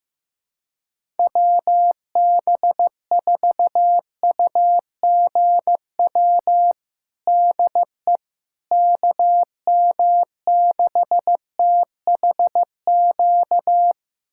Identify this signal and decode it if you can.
Morse code, heard as a keyed tone, about 15 wpm, 700 Hz. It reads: WB4UGW DE KM6THQ
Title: USA Callsign DE Callsign - 6 Characters - 15wpm